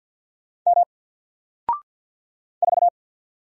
Morse code, heard as a keyed tone, about 50 wpm, 700 Hz. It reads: M 4
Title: Single Letter-Number 50wpm